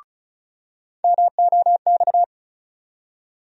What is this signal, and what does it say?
Morse code, heard as a keyed tone, about 36 wpm, 700 Hz. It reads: MOX
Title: USA Call Sign Postfixes - 3 Letters 35wpm